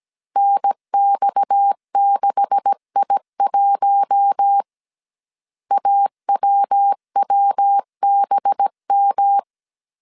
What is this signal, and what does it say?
Morse code, heard as a keyed tone, about 17 wpm, 800 Hz. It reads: NX6I1 AWWBM
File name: CW / DET / 1 9 1500C S A 16wpm 2828